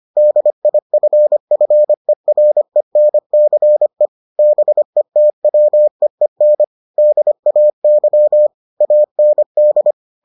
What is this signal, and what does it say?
Morse code, heard as a keyed tone, about 25 wpm, 600 Hz. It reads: DIFFERENCE BETWEEN DAY AND